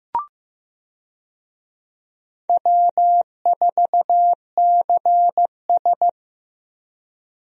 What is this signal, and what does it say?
Morse code, heard as a keyed tone, about 15 wpm, 700 Hz. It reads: W4CS
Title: USA Call Signs - 4 Characters - Repeated 2x - 15wpm